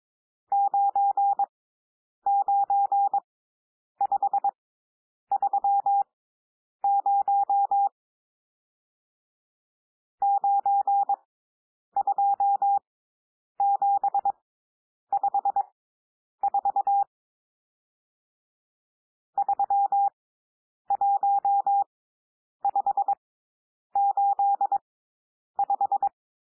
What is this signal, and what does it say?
Morse code, heard as a keyed tone, about 22 wpm, spaced as for 9 wpm, 800 Hz. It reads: 99530 92754 31585
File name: cwmp3 / N1 SOUND NUMBERS 22 wpm